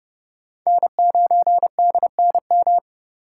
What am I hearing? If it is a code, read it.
Morse code, heard as a keyed tone, about 40 wpm, 700 Hz. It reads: N9DNM